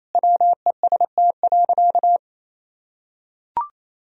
Morse code, heard as a keyed tone, about 27 wpm, 700 Hz. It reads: WEST.
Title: Sentences from Top 1000 Words - Review Part 2 - 28wpm